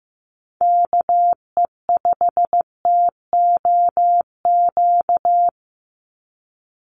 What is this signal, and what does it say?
Morse code, heard as a keyed tone, about 15 wpm, 700 Hz. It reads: KE5TOQ
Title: USA Call Signs 15wpm